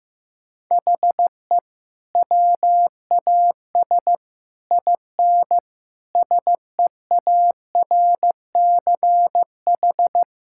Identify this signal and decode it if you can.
Morse code, heard as a keyed tone, about 14 wpm, 700 Hz. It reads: HE WAS IN SEARCH
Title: Sentences from Top 1000 Words - Review Part 5 - 15wpm